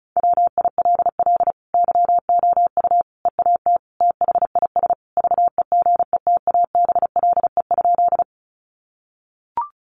Morse code, heard as a keyed tone, about 34 wpm, 700 Hz. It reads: WILL YOU EAT THIS VEGETABLE?
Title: Sentences from Top 1300 Words 35wpm